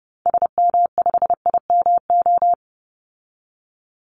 Morse code, heard as a keyed tone, about 30 wpm, 700 Hz. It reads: SM5IMO
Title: Call Signs Repeated 3x - 30wpm